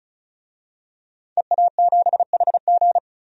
Morse code, heard as a keyed tone, about 34 wpm, 700 Hz. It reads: EA7HG
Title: DXCC-100 Call Signs - 5 Characters - 35wpm